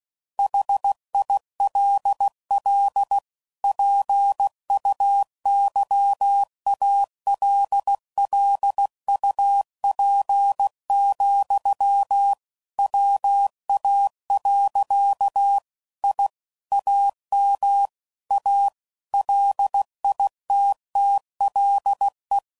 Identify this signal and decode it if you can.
Morse code, heard as a keyed tone, about 16 wpm, 800 Hz. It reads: HILL PUYALLUP, WA. I AM A LITTLE
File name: CW Code Groups @15WPM / QSOs / WA7QOX QRX